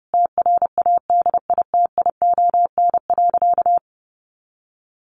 Morse code, heard as a keyed tone, about 30 wpm, 700 Hz. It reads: TRADITION.
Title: Sentences from Top 1500 Words 30wpm